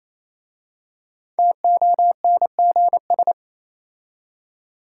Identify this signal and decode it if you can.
Morse code, heard as a keyed tone, about 28 wpm, 690 Hz. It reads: TONGS